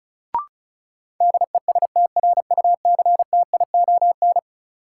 Morse code, heard as a keyed tone, about 34 wpm, 700 Hz. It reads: DESTRUCTION
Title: Sets of 5 Words Encore - New Word Drill - 35wpm